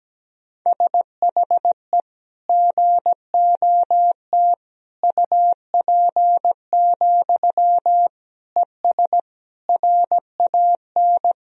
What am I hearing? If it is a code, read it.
Morse code, heard as a keyed tone, about 17 wpm, 700 Hz. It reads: SHE GOT UP, ES RAN
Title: Sentences from Top 500 Words and Abbreviations 17wpm